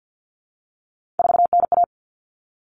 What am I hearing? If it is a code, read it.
Morse code, heard as a keyed tone, about 50 wpm, 700 Hz. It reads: 4NA
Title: Common Cut Numbers - 3-Digit 50wpm